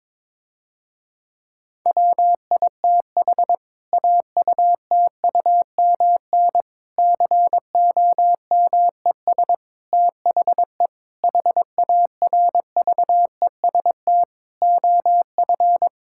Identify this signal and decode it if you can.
Morse code, heard as a keyed tone, about 21 wpm, 700 Hz. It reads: WITH AUTUMN COMES THE HARVEST OF
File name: Difficult Sentences with Hints - 22wpm